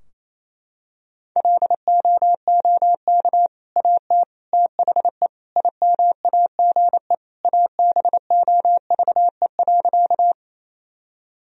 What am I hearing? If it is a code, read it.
Morse code, heard as a keyed tone, about 27 wpm, 700 Hz. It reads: LOOK AT THE IMAGE ABOVE.
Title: Sentences from Top 1000 Words - Review Part 4 - 28wpm